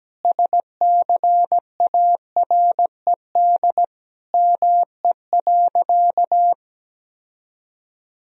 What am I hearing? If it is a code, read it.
Morse code, heard as a keyed tone, about 17 wpm, 700 Hz. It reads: SCARED ME.
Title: Sentences from Top 1000 Words - Review Part 4 - 17wpm